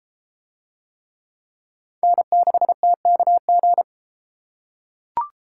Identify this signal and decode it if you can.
Morse code, heard as a keyed tone, about 31 wpm, 700 Hz. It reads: N6TKG